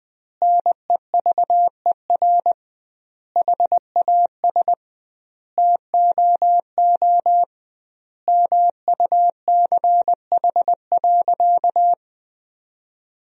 Morse code, heard as a keyed tone, about 21 wpm, 700 Hz. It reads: NEVER HAS TOO MUCH.
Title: 2x Word-Spacing - Sentences from Top 300 Words - 20wpm